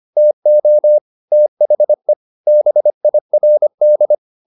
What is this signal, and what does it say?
Morse code, heard as a keyed tone, about 25 wpm, 600 Hz. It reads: TO THE BIRD
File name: the old man and the sea part 1 25wpm